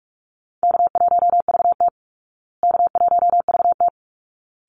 Morse code, heard as a keyed tone, about 44 wpm, 700 Hz. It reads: K1VT K1VT
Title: Easy Call Signs Repeated 2x - 45wpm